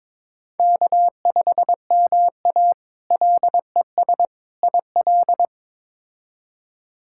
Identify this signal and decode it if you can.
Morse code, heard as a keyed tone, about 22 wpm, 700 Hz. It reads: K5MA LES IL
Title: K1USN SST QSO Practice - 22wpm